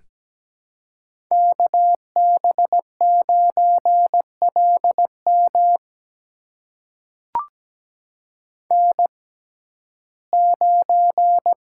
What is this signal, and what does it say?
Morse code, heard as a keyed tone, about 17 wpm, spaced as for 12 wpm, 700 Hz. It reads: KB9LM N9
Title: USA Call Signs - 5 Characters - 6x Character Spacing - 17wpm